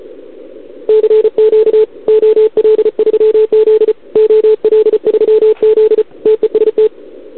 Morse code, heard as a keyed tone, about 34 wpm, 440 Hz. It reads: CQ OL3Z OL3Z TEST
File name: CQWWCW2021 / OL3Z